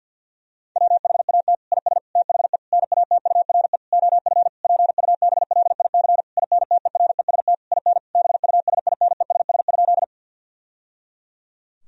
Morse code, heard as a keyed tone, about 50 wpm, 700 Hz. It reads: WHAT IS THE NATURE OF PUBLIC INTEREST IN BUSINESS?
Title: Sentences from Top 1000 Words - Review Part 3 - 50wpm